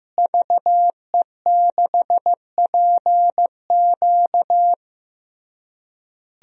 Morse code, heard as a keyed tone, about 15 wpm, 700 Hz. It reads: VE6PQ
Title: DXCC-100 Call Signs - 5 Characters - Repeated 3x - 15wpm